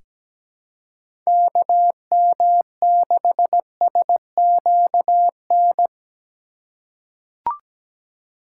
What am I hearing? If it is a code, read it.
Morse code, heard as a keyed tone, about 17 wpm, 700 Hz. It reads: KM6SQN
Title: USA Call Signs - 6 Characters - 2x Character Spacing - 17wpm